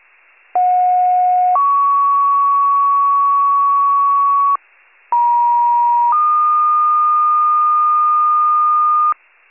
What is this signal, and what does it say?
Two-tone page, heard as a one-second tone, then a three-second tone, about 730 then 1120 Hz, about 950 then 1220 Hz. It is heard from 0.5 to 9.1 s.
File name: pager / 83 RESCUE OFC / wcalert